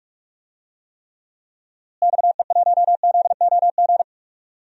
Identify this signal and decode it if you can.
Morse code, heard as a keyed tone, about 42 wpm, 700 Hz. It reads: XE1ZOG
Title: DXCC-100 Call Signs - 6 Characters - Repeated 3x - 45wpm